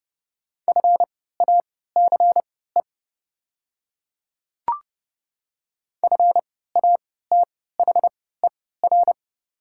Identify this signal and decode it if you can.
Morse code, heard as a keyed tone, about 30 wpm, spaced as for 17 wpm, 690 Hz. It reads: FACE FATHER
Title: 3x Character Spacing - Top 300 Words - 30wpm